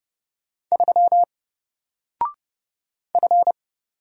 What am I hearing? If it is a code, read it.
Morse code, heard as a keyed tone, about 30 wpm, 700 Hz. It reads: 3 F
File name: Single-Letter-Number-30wpm